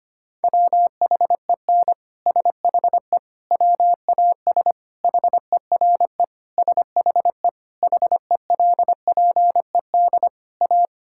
Morse code, heard as a keyed tone, about 25 wpm, 700 Hz. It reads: WHEN SHE WAS HERE SHE HELPED A
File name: practice / 25wpm / Sentences with 3-Word Phrases - 25wpm